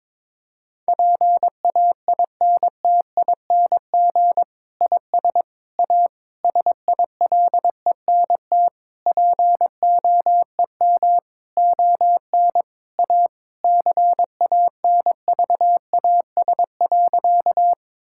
Morse code, heard as a keyed tone, about 22 wpm, 700 Hz. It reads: PAINTING IS A SILENT POEM ON A CANVAS.
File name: Difficult Sentences with Hints - 22wpm